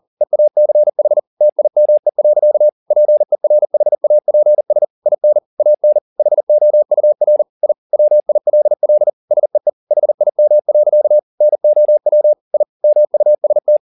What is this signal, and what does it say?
Morse code, heard as a keyed tone, about 40 wpm, 600 Hz. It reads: EACH TIME. PERHAPS IN AN HOUR I WILL SEE HIM. NOW I MUST
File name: the old man and the sea 2 40wpm